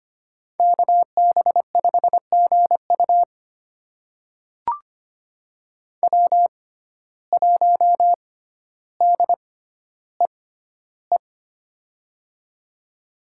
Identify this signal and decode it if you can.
Morse code, heard as a keyed tone, about 25 wpm, spaced as for 13 wpm, 700 Hz. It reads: KB5GU W1DEE
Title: USA Call Signs - 5 Characters - 6x Character Spacing - 25wpm